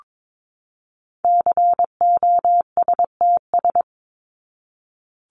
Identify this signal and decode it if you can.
Morse code, heard as a keyed tone, about 22 wpm, 700 Hz. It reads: COSTS
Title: ES:S Suffix - Top 100 Words with Base Words 22wpm